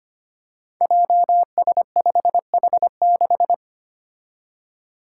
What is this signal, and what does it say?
Morse code, heard as a keyed tone, about 25 wpm, 700 Hz. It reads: JS5H6
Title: Five Challenging Letter-Number Combinations 25wpm